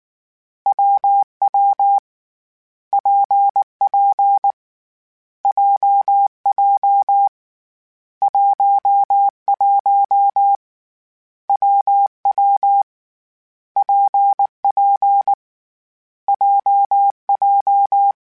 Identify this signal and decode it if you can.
Morse code, heard as a keyed tone, about 19 wpm, 800 Hz. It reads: WW PP JJ 11 WW PP JJ
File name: Difficult Combos / Difficult Combos-mp3s / WPJ WPJ1 @18